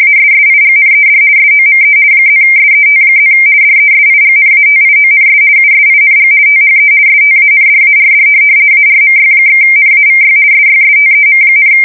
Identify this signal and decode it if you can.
RTTY, about 50 baud. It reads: ASSIGNED SOME +SET+ DESIGNATIONS USING THE STANDARD ARMY/NAVY SYSTEM,
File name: fsk170-50B / teletype-corp